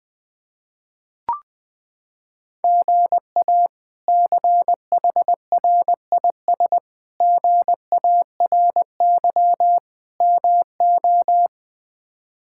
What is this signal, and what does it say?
Morse code, heard as a keyed tone, about 20 wpm, 700 Hz. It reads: GA CHRIS GARY MO
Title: K1USN SST QSO Practice - 20wpm